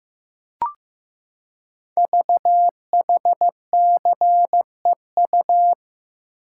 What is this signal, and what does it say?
Morse code, heard as a keyed tone, about 15 wpm, 690 Hz. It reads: VHCEU
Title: Code Groups - 5 Letters - 15wpm